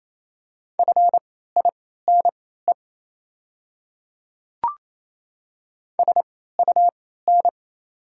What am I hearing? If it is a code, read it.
Morse code, heard as a keyed tone, about 28 wpm, spaced as for 16 wpm, 690 Hz. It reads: FINE SUN
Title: 3x Character Spacing - Top 500 Words - 28wpm